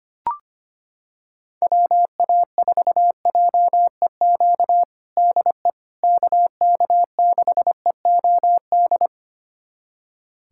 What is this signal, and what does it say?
Morse code, heard as a keyed tone, about 25 wpm, 700 Hz. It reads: WA4JEQ DE KK6EOD
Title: USA Callsign DE Callsign 25wpm